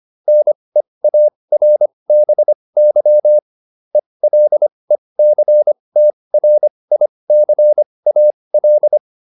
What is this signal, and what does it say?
Morse code, heard as a keyed tone, about 25 wpm, spaced as for 20 wpm, 600 Hz. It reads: NEARBY ELECTRICAL